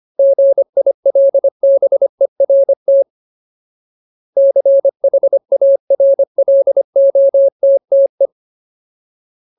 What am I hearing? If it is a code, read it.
Morse code, heard as a keyed tone, about 25 wpm, 550 Hz. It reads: GILBERT CHARLOTTE
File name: Ultimate Word List / 25WPM Ultimate